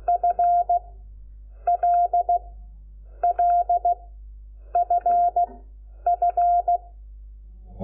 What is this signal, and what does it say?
Morse code, heard as a keyed tone, about 16 wpm, 690 Hz. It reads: F L L F F
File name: Lecciones / CLASE Nº5